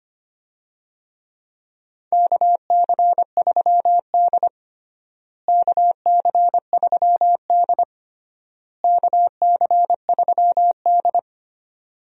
Morse code, heard as a keyed tone, about 25 wpm, 700 Hz. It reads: KC3D KC3D KC3D
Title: USA Call Signs - 4 Characters - Repeated 3x - 25wpm